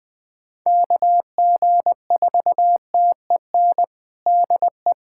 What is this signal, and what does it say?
Morse code, heard as a keyed tone, about 20 wpm, 700 Hz. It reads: KG4TEN DE
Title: USA Callsign DE Callsign - 6 Characters - 20wpm